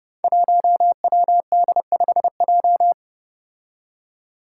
Morse code, heard as a keyed tone, about 30 wpm, 700 Hz. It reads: 1WD5J
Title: Five Challenging Letter-Number Combinations 30wpm